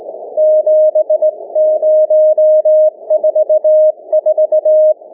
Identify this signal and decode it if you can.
Morse code, heard as a keyed tone, about 20 wpm, 620 Hz. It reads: 7044
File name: cw / ssn 30M